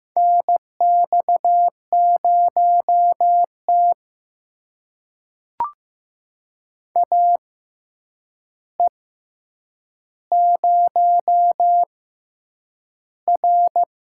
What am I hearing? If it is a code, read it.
Morse code, heard as a keyed tone, about 15 wpm, spaced as for 8 wpm, 700 Hz. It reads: NX0T AE0R
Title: USA Call Signs - 4 Characters - 6x Character Spacing - 15wpm